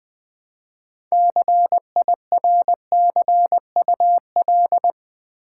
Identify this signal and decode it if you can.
Morse code, heard as a keyed tone, about 20 wpm, 700 Hz. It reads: CIRCUL
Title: Build A Word - 3 Letters At A Time - 9-Letter Words - 20wpm